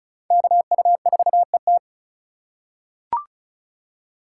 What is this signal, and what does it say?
Morse code, heard as a keyed tone, about 36 wpm, 700 Hz. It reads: KU4ET
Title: USA Call Signs - 4x Character Spacing - 35wpm